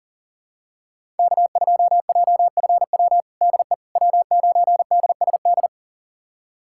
Morse code, heard as a keyed tone, about 40 wpm, 700 Hz. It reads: K2JFW DE W9DSD
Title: USA Callsign DE Callsign - 5 Characters - 40wpm